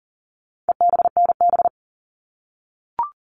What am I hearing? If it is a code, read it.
Morse code, heard as a keyed tone, about 40 wpm, 700 Hz. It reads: EBNB
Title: All Cut Numbers - 4-Digit 40wpm